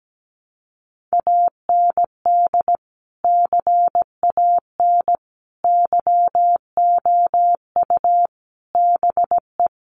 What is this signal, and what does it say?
Morse code, heard as a keyed tone, about 17 wpm, 700 Hz. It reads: AND CAN YOU BE
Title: Sentences from Top 1100 Words 17wpm